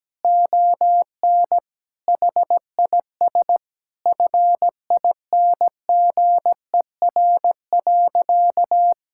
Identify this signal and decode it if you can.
Morse code, heard as a keyed tone, about 17 wpm, 700 Hz. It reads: ON HIS FINGER.
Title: Sentences from Top 1600 Words 17wpm